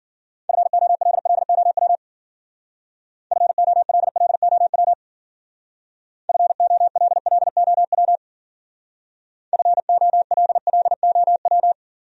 Morse code, heard as a keyed tone, about 51 wpm, 700 Hz. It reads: FOLLOW FOLLOW FOLLOW FOLLOW